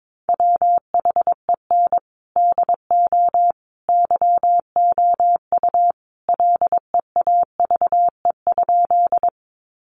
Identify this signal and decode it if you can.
Morse code, heard as a keyed tone, about 22 wpm, 700 Hz. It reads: WHEN DO YOU LEAVE?
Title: Sentences from Top 400 Words 22wpm